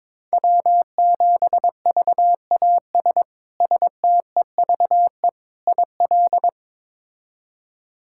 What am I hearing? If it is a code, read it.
Morse code, heard as a keyed tone, about 22 wpm, 700 Hz. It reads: W7VAS STEVE IL
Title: K1USN SST QSO Practice - 22wpm